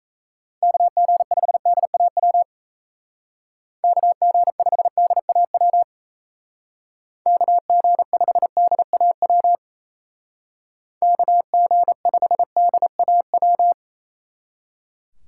Morse code, heard as a keyed tone, about 32 wpm, 700 Hz. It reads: KG5DAW KG5DAW KG5DAW KG5DAW
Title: Speed Racing - USA 2x3 Call Signs 28wpm